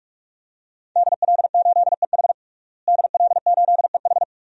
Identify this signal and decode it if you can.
Morse code, heard as a keyed tone, about 46 wpm, 700 Hz. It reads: DL8EH DL8EH
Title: DXCC-100 Call Signs - 5 Characters - Repeated 2x - 45wpm